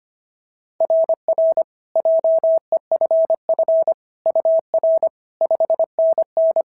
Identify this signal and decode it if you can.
Morse code, heard as a keyed tone, about 25 wpm, 690 Hz. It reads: RR JEFF UR 5NN
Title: SOTA QSO Practice 25wpm